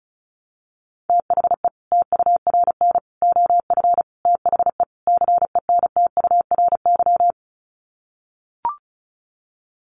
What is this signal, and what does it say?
Morse code, heard as a keyed tone, about 34 wpm, 700 Hz. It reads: THE TURN OF THE CENTURY
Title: Sets of 5 Words 35wpm